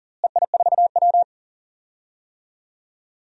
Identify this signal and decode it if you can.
Morse code, heard as a keyed tone, about 39 wpm, 700 Hz. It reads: EI4W
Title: DXCC-100 Call Signs - 4 Characters - Repeated 3x - 40wpm